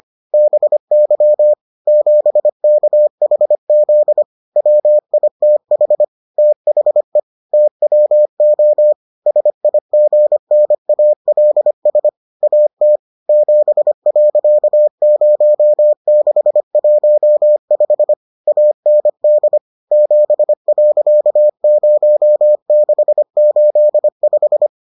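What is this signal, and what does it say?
Morse code, heard as a keyed tone, about 25 wpm, 600 Hz. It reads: BY 7KHZ WITH THE TWO SIGNALS AT 7.0615 AND 7.0685
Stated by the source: Morse edition of the Foundations of Amateur Radio episodes.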